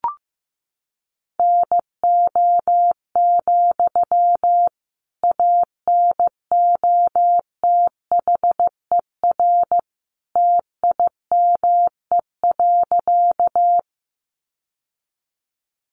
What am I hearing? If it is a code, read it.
Morse code, heard as a keyed tone, about 15 wpm, 700 Hz. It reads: NO, ANOTHER TIME.
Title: Sentences from Top 600 Words 15wpm